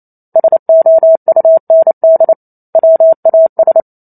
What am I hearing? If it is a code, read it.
Morse code, heard as a keyed tone, about 28 wpm, 650 Hz. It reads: SOUND WAS
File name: Chapter 2 - 28 WPM